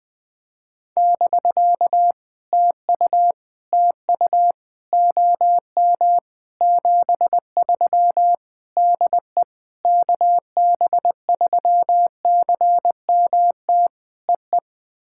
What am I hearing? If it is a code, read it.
Morse code, heard as a keyed tone, about 20 wpm, 700 Hz. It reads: <BK> TU TU OM 73 DE KB3CMT EE
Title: POTA QSO Practice - 20wpm